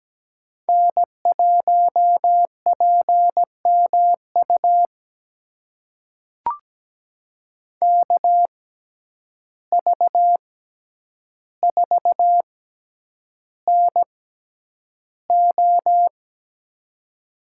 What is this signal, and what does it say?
Morse code, heard as a keyed tone, about 17 wpm, spaced as for 9 wpm, 700 Hz. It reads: N1PMU KV4NO
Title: USA Call Signs - 5 Characters - 6x Character Spacing - 17wpm